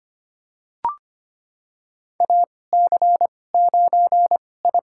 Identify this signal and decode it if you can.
Morse code, heard as a keyed tone, about 25 wpm, spaced as for 18 wpm, 700 Hz. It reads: AC9I
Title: USA Call Signs - 2x Character Spacing - 25wpm